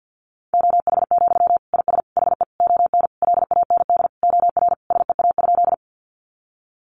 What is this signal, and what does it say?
Morse code, heard as a keyed tone, about 50 wpm, 700 Hz. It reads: OH, IS HE ON LAND OR SEA?
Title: Sentences from Top 500 Words 50wpm